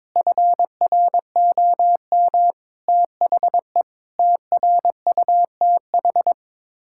Morse code, heard as a keyed tone, about 22 wpm, 700 Hz. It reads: FROM THE TRUTH
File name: practice / 22wpm / Sets of 8 Words - 22wpm